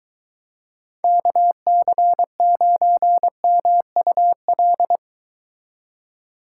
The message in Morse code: KC9MUL

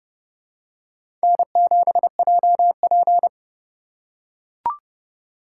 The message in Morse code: N7JP

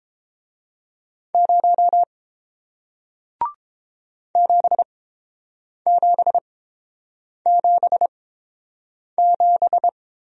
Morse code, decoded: 0 7777

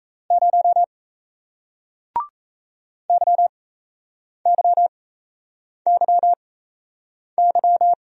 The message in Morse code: 0 YYYY

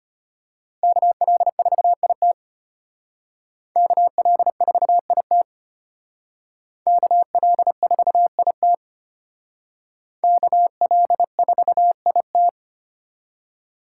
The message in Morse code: KL4IT KL4IT KL4IT KL4IT

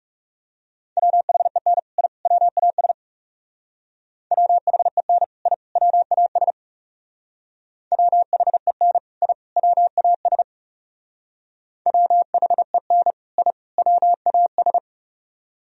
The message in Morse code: WHEN I WAS WHEN I WAS WHEN I WAS WHEN I WAS

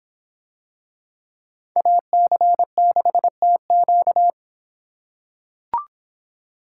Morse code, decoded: AC6TQ

